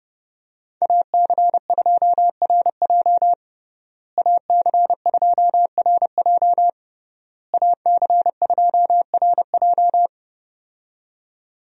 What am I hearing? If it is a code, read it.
Morse code, heard as a keyed tone, about 30 wpm, 700 Hz. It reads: AC2RJ AC2RJ AC2RJ
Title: USA Call Signs - 5 Characters - Repeated 3x - 30wpm